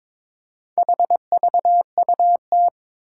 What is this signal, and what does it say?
Morse code, heard as a keyed tone, about 21 wpm, 700 Hz. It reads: HVUT
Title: Four Challenging Letter-Number Combinations 22wpm